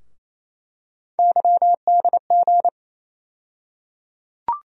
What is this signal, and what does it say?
Morse code, heard as a keyed tone, about 28 wpm, 700 Hz. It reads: YDG
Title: USA Call Sign Postfixes - 3 Letters 28wpm